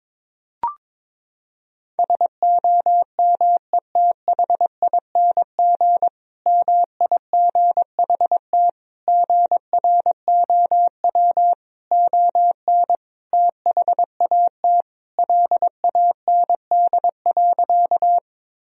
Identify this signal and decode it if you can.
Morse code, heard as a keyed tone, about 22 wpm, 700 Hz. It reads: SOMETHING MIGHT GROW ON THAT LAND.
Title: Sentences from Top 1000 Words - Review Part 1 - 22wpm